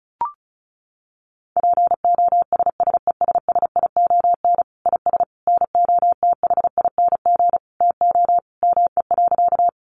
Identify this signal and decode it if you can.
Morse code, heard as a keyed tone, about 34 wpm, 700 Hz. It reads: POSSESSION IS NOTHING TO ME.
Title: Sentences from Top 1400 Words 35wpm